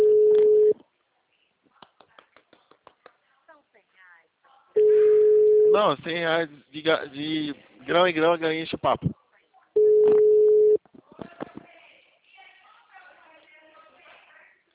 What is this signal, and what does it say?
Ringback tone (Europe, 425 Hz), with speech.